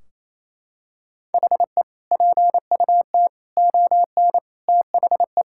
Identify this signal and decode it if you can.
Morse code, heard as a keyed tone, about 28 wpm, 700 Hz. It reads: HE PUT ON THE
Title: Sentences from Top 1600 Words 28wpm